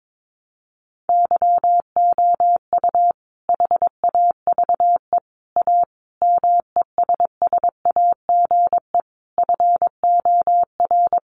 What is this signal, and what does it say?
Morse code, heard as a keyed tone, about 22 wpm, 700 Hz. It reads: YOU HAVE A MESSAGE FOR
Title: Sentences from Top 1100 Words 22wpm